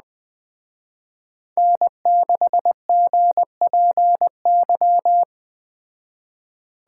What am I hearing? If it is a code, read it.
Morse code, heard as a keyed tone, about 20 wpm, 700 Hz. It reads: N6GPY
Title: Speed Racing - USA 1x3 Call Signs 17wpm